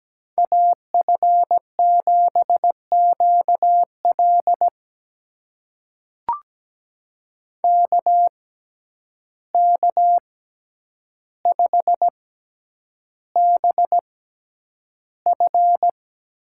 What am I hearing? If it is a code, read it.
Morse code, heard as a keyed tone, about 17 wpm, spaced as for 9 wpm, 700 Hz. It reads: AF7QL KK5BF
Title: USA Call Signs - 5 Characters - 6x Character Spacing - 17wpm